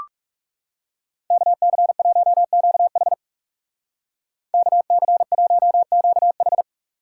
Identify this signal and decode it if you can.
Morse code, heard as a keyed tone, about 35 wpm, 700 Hz. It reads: KC1QH KC1QH
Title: Speed Racing - USA Call Signs 30wpm